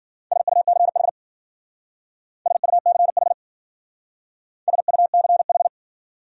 Morse code, heard as a keyed tone, about 50 wpm, 700 Hz. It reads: SUCH SUCH SUCH
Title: Speed-Racing Top 500 Words 40wpm